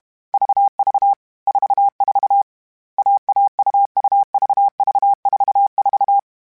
Morse code, heard as a keyed tone, about 32 wpm, 800 Hz. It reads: VV 44 AAUUVV44